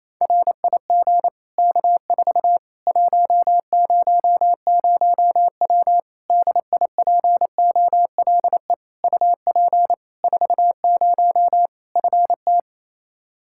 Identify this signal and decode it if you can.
Morse code, heard as a keyed tone, about 28 wpm, 700 Hz. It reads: RIG K4 100W DIPOLE UP 40 FT
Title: Practice QSO - Exchange 2 - Info - 28wpm